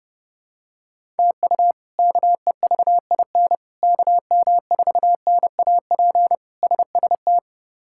Morse code, heard as a keyed tone, about 30 wpm, 700 Hz. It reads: TU KEVIN KM4NAP SST
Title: K1USN SST QSO Practice - 30wpm